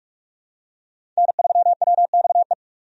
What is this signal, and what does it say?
Morse code, heard as a keyed tone, about 43 wpm, 700 Hz. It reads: N3WXE
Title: USA Call Signs - 5 Characters - 45wpm